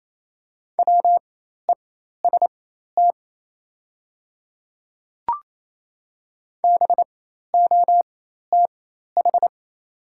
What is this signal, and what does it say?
Morse code, heard as a keyed tone, about 27 wpm, spaced as for 13 wpm, 690 Hz. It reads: WEST BOTH
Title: 4x Character Spacing - Top 500 Words - 28wpm